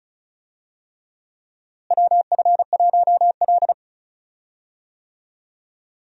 Morse code, heard as a keyed tone, about 36 wpm, 700 Hz. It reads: WF1L
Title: USA Call Signs - 4 Characters - Repeated 3x - 35wpm